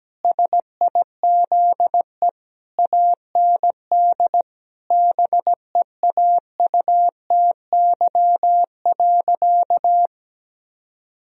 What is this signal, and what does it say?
Morse code, heard as a keyed tone, about 17 wpm, 700 Hz. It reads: SIZE AND BEAUTY.
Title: Sentences from Top 1000 Words - Review Part 2 - 17wpm